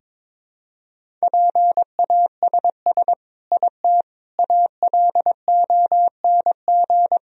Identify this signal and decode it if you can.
Morse code, heard as a keyed tone, about 21 wpm, 700 Hz. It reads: PASS IT ALONG
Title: Sentences from Top 1000 Words - Review Part 1 - 22wpm